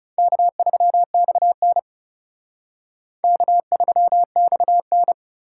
Morse code, heard as a keyed tone, about 37 wpm, 700 Hz. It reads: K3XN K3XN
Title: Speed Racing - USA 1x2 Call Signs 30wpm